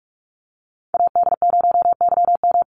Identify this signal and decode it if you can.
Morse code, heard as a keyed tone, about 46 wpm, 700 Hz. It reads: AD0YM
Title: NA SOTA Chaser Callsigns 45wpm